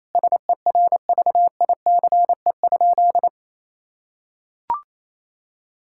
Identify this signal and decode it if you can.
Morse code, heard as a keyed tone, about 28 wpm, 700 Hz. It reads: SERVICE?
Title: Sentences from Top 1000 Words - Review Part 6 - 28wpm